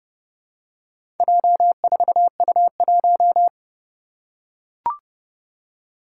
Morse code, heard as a keyed tone, about 29 wpm, 700 Hz. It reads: J4U1
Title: Four Challenging Letter-Number Combinations 30wpm